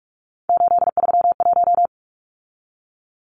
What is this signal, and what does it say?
Morse code, heard as a keyed tone, about 44 wpm, 700 Hz. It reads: 831
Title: Numbers - 3-Digit 45wpm